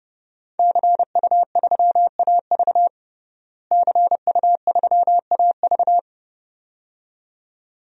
Morse code, heard as a keyed tone, about 30 wpm, 700 Hz. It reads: CU3AV CU3AV
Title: DXCC-100 Call Signs - 5 Characters - Repeated 2x - 30wpm